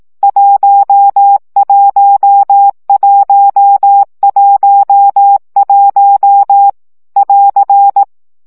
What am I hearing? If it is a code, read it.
Morse code, heard as a keyed tone, about 18 wpm, 800 Hz. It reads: 11111 <AR>
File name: CW gyakorlas 1 15 baud